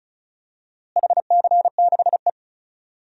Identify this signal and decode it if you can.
Morse code, heard as a keyed tone, about 34 wpm, 700 Hz. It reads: HC6E
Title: DXCC-100 Call Signs - 4 Characters - Repeated 3x - 35wpm